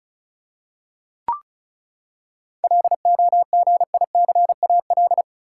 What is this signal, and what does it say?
Morse code, heard as a keyed tone, about 34 wpm, 700 Hz. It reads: LOGICAL
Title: Seven Letter Words - 35wpm